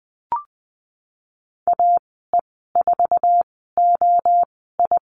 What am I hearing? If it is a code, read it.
Morse code, heard as a keyed tone, about 20 wpm, spaced as for 15 wpm, 700 Hz. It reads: AE4OI
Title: 076 Lesson 13g - Letter L - Call Signs 20wpm